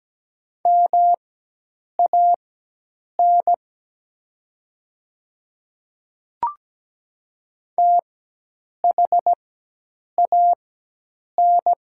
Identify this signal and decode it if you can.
Morse code, heard as a keyed tone, about 17 wpm, spaced as for 8 wpm, 700 Hz. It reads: MAN THAN
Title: 4x Character Spacing - Top 200 Words - 17wpm